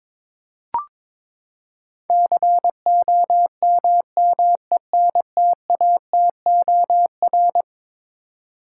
Morse code, heard as a keyed tone, about 22 wpm, 700 Hz. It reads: COMMENTATOR
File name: practice / 22wpm / Eleven Letter Words - 22wpm